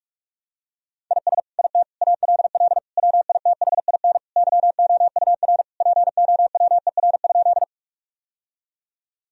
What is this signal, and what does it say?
Morse code, heard as a keyed tone, about 41 wpm, 700 Hz. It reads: IS IT ALL WITHIN YOUR POWER?
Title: Sentences from Top 1000 Words - Review Part 2 - 45wpm